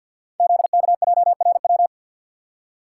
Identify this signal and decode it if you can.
Morse code, heard as a keyed tone, about 50 wpm, 700 Hz. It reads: ZKJRW
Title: Code Groups - 5 Letter:Numbers - 50wpm